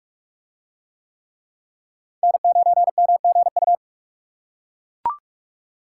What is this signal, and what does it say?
Morse code, heard as a keyed tone, about 41 wpm, 700 Hz. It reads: N9MGU